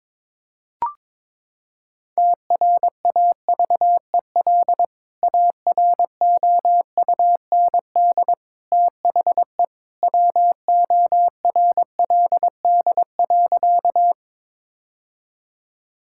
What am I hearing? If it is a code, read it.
Morse code, heard as a keyed tone, about 21 wpm, 700 Hz. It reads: TRAVEL AROUND THE WORLD.
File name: practice / 22wpm / Sentences with 3-Word Phrases - 22wpm